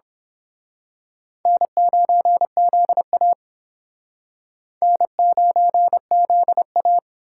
Morse code, heard as a keyed tone, about 26 wpm, 700 Hz. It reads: N9ZA N9ZA
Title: Speed Racing - USA 1x2 Call Signs 22wpm